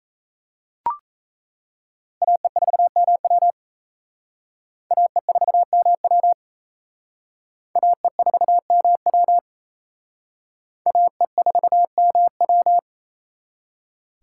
Morse code, heard as a keyed tone, about 34 wpm, 700 Hz. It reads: AE4MW AE4MW AE4MW AE4MW
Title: Speed Racing - USA Call Signs 28wpm